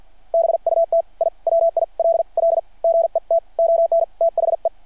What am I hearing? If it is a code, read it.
Morse code, heard as a keyed tone, about 53 wpm, spaced as for 44 wpm, 660 Hz. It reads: BUT I WILL GET ON THE
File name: archived / CRSnet